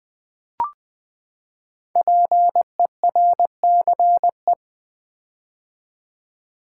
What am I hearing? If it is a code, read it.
Morse code, heard as a keyed tone, about 20 wpm, 700 Hz. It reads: PERCE
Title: Build A Word - 5 Letters At A Time - 10-Letter Words - 20wpm